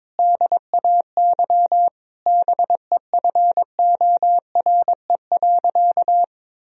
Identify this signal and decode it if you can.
Morse code, heard as a keyed tone, about 22 wpm, 700 Hz. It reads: DAY BEFORE.
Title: Sentences from Top 1000 Words - Review Part 1 - 22wpm